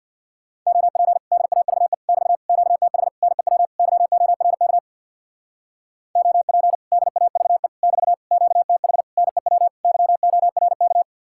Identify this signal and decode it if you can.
Morse code, heard as a keyed tone, about 60 wpm, 700 Hz. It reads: OP DAVE <BT> QTH NEW YORK OP DAVE <BT> QTH NEW YORK
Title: Speed-Racing Common QSO Phrases - 50wpm